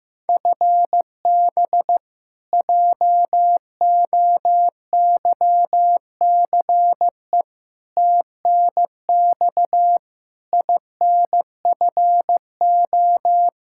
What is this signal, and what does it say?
Morse code, heard as a keyed tone, about 15 wpm, 700 Hz. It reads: FB JOYCE TNX INFO
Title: K1USN SST QSO Practice 15wpm